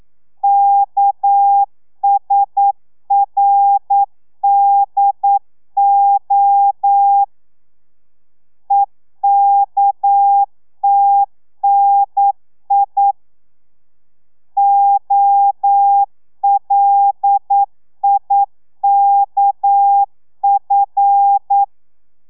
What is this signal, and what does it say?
Morse code, heard as a keyed tone, about 9 wpm, 800 Hz. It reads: KSRDO EKTNI OLIKF
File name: telegrafia / i2tfj / Lez6 40